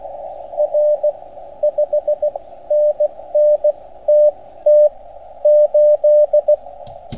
Morse code, heard as a keyed tone, about 16 wpm, spaced as for 12 wpm, 600 Hz, with radio noise. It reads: R 5NN TT8
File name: Archiv / CUC287 / OK1FGD CUC 287